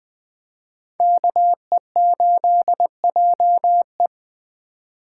Morse code, heard as a keyed tone, about 20 wpm, 700 Hz. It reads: KE8JE
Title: Speed Racing - USA 2x2 Call Signs 17wpm